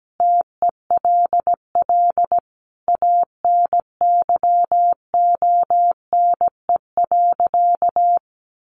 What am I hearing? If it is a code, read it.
Morse code, heard as a keyed tone, about 17 wpm, 700 Hz. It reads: TELL ANYONE.
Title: Sentences from Top 900 Words 17wpm